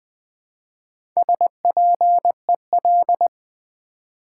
Morse code, heard as a keyed tone, about 20 wpm, 700 Hz. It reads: SPEL
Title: Build A Word - 1 Letter At A Time - 5-Letter Words - 20wpm